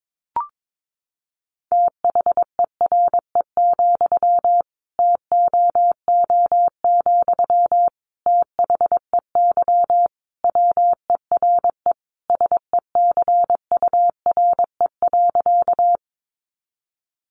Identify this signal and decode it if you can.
Morse code, heard as a keyed tone, about 22 wpm, 700 Hz. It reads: THERE, TOO, THEY WERE SECURE.
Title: Sentences from Top 1200 Words 22wpm